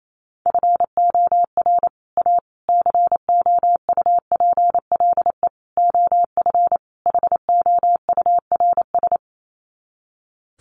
Morse code, heard as a keyed tone, about 28 wpm, 700 Hz. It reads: FOR A COUPLE OF HOURS
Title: Sets of 5 Words Encore2 28wpm